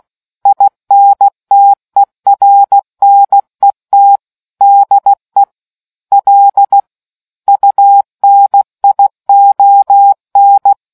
Morse code, heard as a keyed tone, about 16 wpm, 800 Hz. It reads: INTERNET DE L UNION